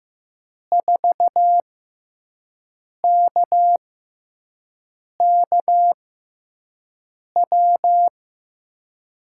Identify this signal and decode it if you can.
Morse code, heard as a keyed tone, about 15 wpm, spaced as for 5 wpm, 700 Hz. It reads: 4KKW
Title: USA Call Signs - 6 Characters - 6x Character Spacing - 15wpm